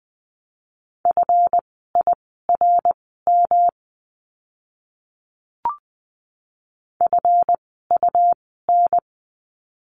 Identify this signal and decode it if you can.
Morse code, heard as a keyed tone, about 20 wpm, spaced as for 14 wpm, 700 Hz. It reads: FIRM FUN
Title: 139 Lesson 23f - Letter f - Words 20wpm